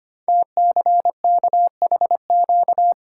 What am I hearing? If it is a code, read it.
Morse code, heard as a keyed tone, about 25 wpm, 690 Hz. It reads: TCKHQ
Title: Code Groups - 5 Letters - 25wpm